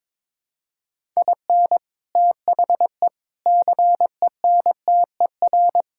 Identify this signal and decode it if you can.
Morse code, heard as a keyed tone, about 22 wpm, 700 Hz. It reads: IN THE CENTER
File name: practice / 22wpm / Sentences with 4-Word Phrases - 22wpm